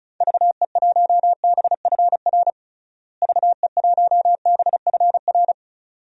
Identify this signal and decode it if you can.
Morse code, heard as a keyed tone, about 34 wpm, 700 Hz. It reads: VE1BFR VE1BFR
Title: DXCC-100 Call Signs - 6 Characters - Repeated 2x - 35wpm